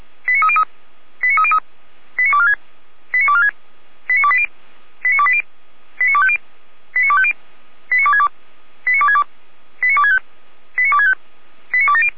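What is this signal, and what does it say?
Five-tone selective call (ZVEI), heard as quick runs of five tones, tones of about 70 ms. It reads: 89303 89303 89246 89246 89280 89280 89250 89250 89262 89262 89265 89265 89270
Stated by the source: ZVEI five-tone selective call received by SDR.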